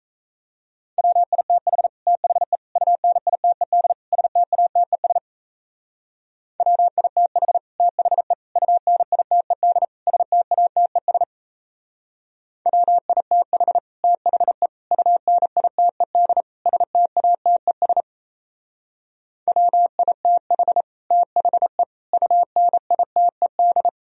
Morse code, heard as a keyed tone, about 36 wpm, 700 Hz. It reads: WITH THE UNITED STATES WITH THE UNITED STATES WITH THE UNITED STATES WITH THE UNITED